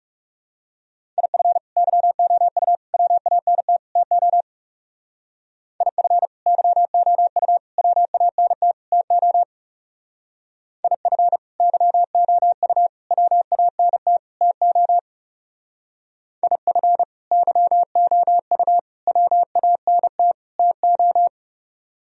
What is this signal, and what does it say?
Morse code, heard as a keyed tone, about 35 wpm, 700 Hz. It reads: IF YOU WANT TO IF YOU WANT TO IF YOU WANT TO IF YOU WANT TO
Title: Speed-Racing - Sets of 4 Words 30wpm